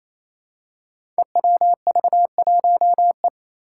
Morse code, heard as a keyed tone, about 27 wpm, 700 Hz. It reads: EWV1E
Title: Five Challenging Letter-Number Combinations 28wpm